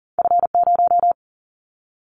Morse code, heard as a keyed tone, about 40 wpm, 700 Hz. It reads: F0
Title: Callsign Prefixes for Top 25 DXCC - 40wpm